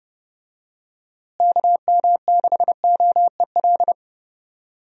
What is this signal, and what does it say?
Morse code, heard as a keyed tone, about 30 wpm, 700 Hz. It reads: KM6OEL
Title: USA Call Signs - 6 Characters - Repeated 3x - 30wpm